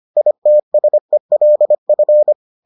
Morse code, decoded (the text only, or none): ITSELF